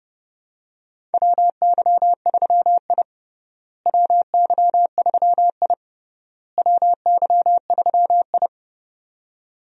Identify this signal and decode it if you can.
Morse code, heard as a keyed tone, about 30 wpm, 700 Hz. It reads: WY3I WY3I WY3I
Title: USA Call Signs - 4 Characters - Repeated 3x - 30wpm